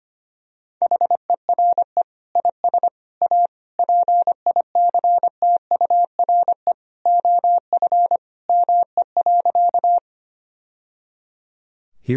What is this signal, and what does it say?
Morse code, heard as a keyed tone, about 25 wpm, 700 Hz. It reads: HERE IS A PICTURE OF ME.
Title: Sentences from Top 1000 Words - Review Part 1 - 25wpm